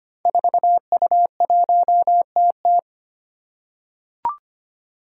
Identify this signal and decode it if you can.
Morse code, heard as a keyed tone, about 25 wpm, 700 Hz. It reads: 4U1TT
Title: Five Challenging Letter-Number Combinations 25wpm